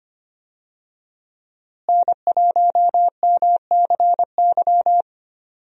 Morse code, heard as a keyed tone, about 25 wpm, 700 Hz. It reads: N1MCY
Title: USA Call Signs - 5 Characters - Repeated 3x - 25wpm